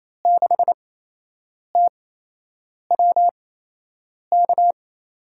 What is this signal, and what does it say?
Morse code, heard as a keyed tone, about 28 wpm, spaced as for 8 wpm, 700 Hz. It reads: 6TWK
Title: USA Call Signs - 8x Character Spacing - 28wpm